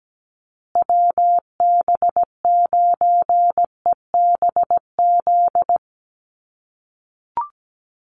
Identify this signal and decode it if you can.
Morse code, heard as a keyed tone, about 17 wpm, 700 Hz. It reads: WB9EBZ
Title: USA Call Signs Repeated 3x 17wpm